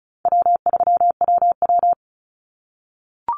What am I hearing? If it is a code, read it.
Morse code, heard as a keyed tone, about 36 wpm, 700 Hz. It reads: W3WW
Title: Call Signs - 35wpm